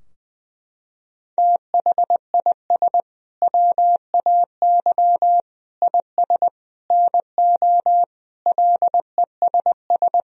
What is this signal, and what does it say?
Morse code, heard as a keyed tone, about 21 wpm, 700 Hz. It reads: THIS WAY IS NO LESS
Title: Sentences from Top 1000 Words - Review Part 2 - 20wpm